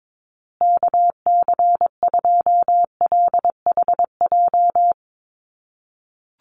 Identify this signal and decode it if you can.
Morse code, heard as a keyed tone, about 22 wpm, 700 Hz. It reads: KC2LHJ
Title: USA Call Signs Repeated 2x 22wpm